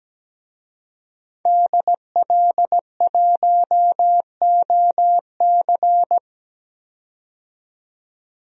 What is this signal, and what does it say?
Morse code, heard as a keyed tone, about 17 wpm, 700 Hz. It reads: DL1OC
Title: DXCC-100 Call Signs - 5 Characters - 17wpm